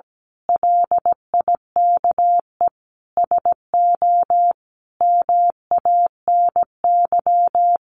Morse code, heard as a keyed tone, about 17 wpm, 700 Hz. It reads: LIKE SO MANY
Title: A Sound Of Thunder 17wpm